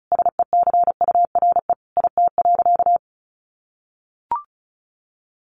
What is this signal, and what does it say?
Morse code, heard as a keyed tone, about 34 wpm, 700 Hz. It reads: SECURE IT.
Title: Sentences from Top 1200 Words 35wpm